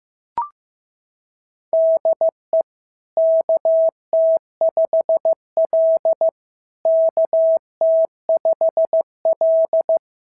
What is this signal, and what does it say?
Morse code, heard as a keyed tone, about 15 wpm, 650 Hz. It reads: DE KT5L KT5L
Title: SOTA QSO Practice 15wpm